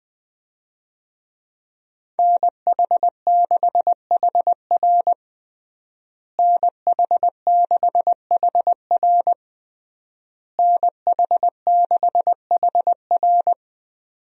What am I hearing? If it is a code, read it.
Morse code, heard as a keyed tone, about 21 wpm, 700 Hz. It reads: NH6HR NH6HR NH6HR
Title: USA Call Signs - 5 Characters - Repeated 3x - 20wpm